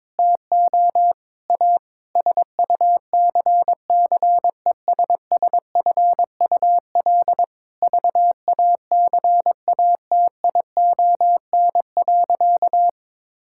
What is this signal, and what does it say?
Morse code, heard as a keyed tone, about 22 wpm, 700 Hz. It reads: TO A SUCCESSFUL VACATION.
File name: Difficult Sentences with Hints - 22wpm